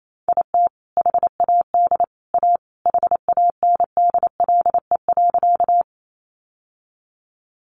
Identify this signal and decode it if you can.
Morse code, heard as a keyed tone, about 28 wpm, 700 Hz. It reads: IT HAD A HANDLE.
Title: Sentences from Top 1100 Words 28wpm